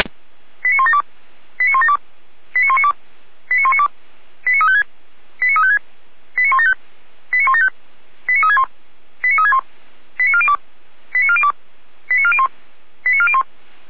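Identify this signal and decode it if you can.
Five-tone selective call (ZVEI), heard as quick runs of five tones, tones of about 70 ms. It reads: 89172 89172 89192 89192 89356 89356 89175 89175 89361 89361 89402 89402 89401 89401
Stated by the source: ZVEI five-tone selective call received on SDR, from Feuerwehr Orlamünde fire brigade.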